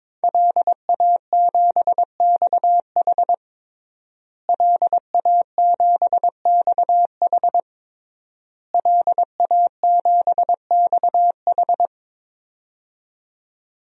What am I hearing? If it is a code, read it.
Morse code, heard as a keyed tone, about 22 wpm, 700 Hz. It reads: LA7XH LA7XH LA7XH
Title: DXCC-100 Call Signs - 5 Characters - Repeated 3x - 22wpm